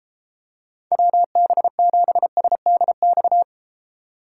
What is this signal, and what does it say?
Morse code, heard as a keyed tone, about 33 wpm, 700 Hz. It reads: WB7SDX